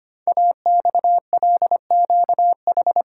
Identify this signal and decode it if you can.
Morse code, heard as a keyed tone, about 25 wpm, 700 Hz. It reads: AXLQH
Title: Code Groups - 5 Letters - 25wpm